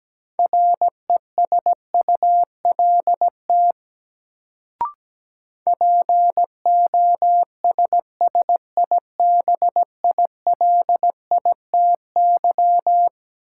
Morse code, heard as a keyed tone, about 17 wpm, 700 Hz. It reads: RESULT POSSIBILITY
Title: Sets of 3 Words - New Word Drill - 17wpm